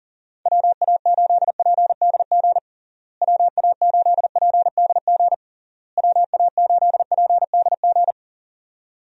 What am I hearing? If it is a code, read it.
Morse code, heard as a keyed tone, about 40 wpm, 700 Hz. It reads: WA8PDG WA8PDG WA8PDG
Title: USA Call Signs - 6 Characters - Repeated 3x - 40wpm